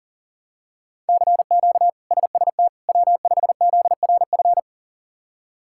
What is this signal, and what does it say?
Morse code, heard as a keyed tone, about 40 wpm, 700 Hz. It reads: CQ SST W5ZRF
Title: K1USN SST QSO Practice - 40wpm